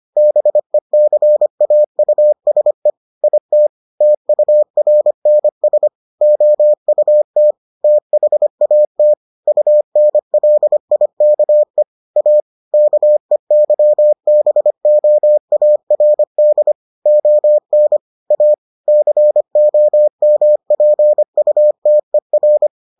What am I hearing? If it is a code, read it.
Morse code, heard as a keyed tone, about 25 wpm, 600 Hz. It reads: BECAUSE IT TURNS OUT THAT UNLIKE A KEYBOARD ON A COMPUTER